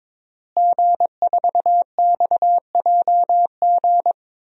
Morse code, heard as a keyed tone, about 22 wpm, 700 Hz. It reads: G4XJG
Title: DXCC-100 Call Signs - 5 Characters - Repeated 3x - 22wpm